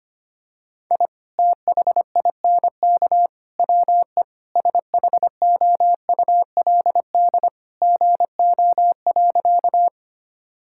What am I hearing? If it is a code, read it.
Morse code, heard as a keyed tone, about 25 wpm, 700 Hz. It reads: I THINK WE SHOULD GO.
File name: practice / 25wpm / Sentences with 3-Word Phrases - 25wpm